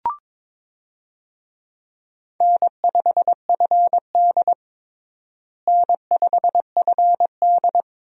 Morse code, heard as a keyed tone, about 22 wpm, 700 Hz. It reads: N5FD N5FD
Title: USA Call Signs - 4 Characters - Repeated 2x - 22wpm